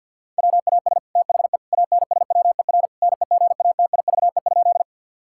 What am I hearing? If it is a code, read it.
Morse code, heard as a keyed tone, about 50 wpm, 700 Hz. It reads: WAS THE ANSWER NEGATIVE?
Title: Sentences from Top 1000 Words - Review Part 6 - 50wpm